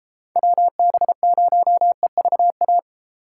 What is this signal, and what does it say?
Morse code, heard as a keyed tone, about 37 wpm, 700 Hz. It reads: WB0EVA